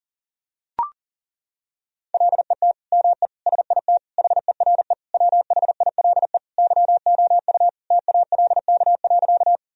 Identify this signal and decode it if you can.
Morse code, heard as a keyed tone, about 40 wpm, 700 Hz. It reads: LET ME SIT HERE WHILE YOU TALK.
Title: Sentences from Top 1000 Words - Review Part 1 - 40wpm